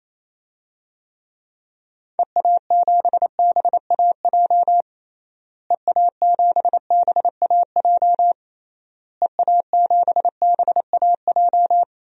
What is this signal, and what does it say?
Morse code, heard as a keyed tone, about 27 wpm, 700 Hz. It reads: EA7BAJ EA7BAJ EA7BAJ
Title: DXCC-100 Call Signs - 6 Characters - Repeated 3x - 28wpm